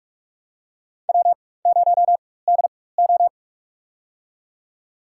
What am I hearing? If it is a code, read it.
Morse code, heard as a keyed tone, about 43 wpm, spaced as for 21 wpm, 700 Hz. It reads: W0DO
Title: USA Call Signs - 4 Characters - 4x Character Spacing - 45wpm